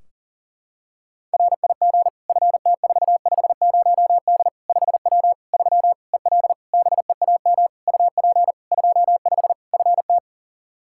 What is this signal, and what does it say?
Morse code, heard as a keyed tone, about 40 wpm, 700 Hz. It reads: RIG FT450D 5W 3 EL BEAM UP 25 FT
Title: Practice QSO - Exchange 2 - Info - 40wpm